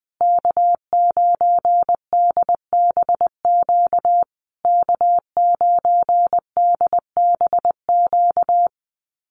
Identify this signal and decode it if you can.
Morse code, heard as a keyed tone, about 20 wpm, 700 Hz. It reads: K9DBQ K9DBQ
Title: USA Call Signs Repeated 3x 20wpm